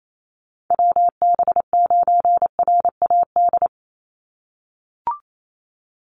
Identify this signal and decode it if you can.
Morse code, heard as a keyed tone, about 28 wpm, 700 Hz. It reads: WB9RAD
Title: USA Call Signs 28wpm